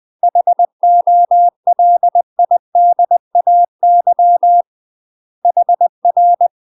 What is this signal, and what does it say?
Morse code, heard as a keyed tone, about 20 wpm, 700 Hz. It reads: HOLIDAY HR
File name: UK MASTER QSOs TRAINING / 20WPM / UK MASTER QSOs 0063